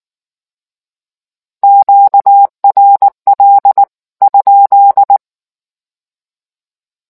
Morse code, heard as a keyed tone, about 19 wpm, 800 Hz. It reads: QRL ?